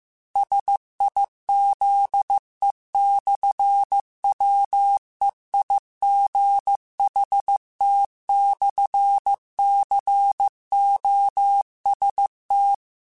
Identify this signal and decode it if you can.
Morse code, heard as a keyed tone, about 15 wpm, 800 Hz. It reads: SIZE/WEIGHT/COST
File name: Technical Subjects / Technical Subjects-mp3s / Sw Pwr Sup @13